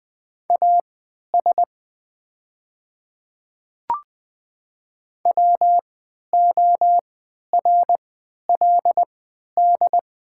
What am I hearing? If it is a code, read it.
Morse code, heard as a keyed tone, about 20 wpm, spaced as for 11 wpm, 690 Hz. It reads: AS WORLD
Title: 3x Character Spacing - Top 200 Words - 20wpm